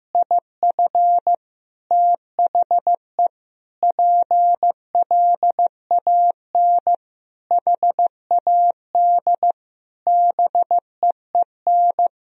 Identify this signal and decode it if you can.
Morse code, heard as a keyed tone, about 15 wpm, 700 Hz. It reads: IF THE PLAN HAD BEEN
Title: Sentences from Top 1000 Words - Review Part 2 - 15wpm